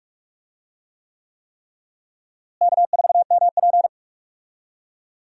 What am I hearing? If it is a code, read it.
Morse code, heard as a keyed tone, about 48 wpm, 700 Hz. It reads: K4MP
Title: USA Call Signs - 4 Characters - Repeated 3x - 45wpm